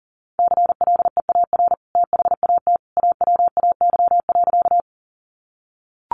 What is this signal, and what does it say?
Morse code, heard as a keyed tone, about 37 wpm, 700 Hz. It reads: CLEAR THAT AWAY.
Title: Sentences from Top 600 Words 40wpm